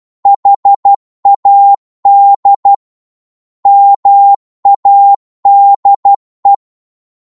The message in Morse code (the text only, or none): HAD MADE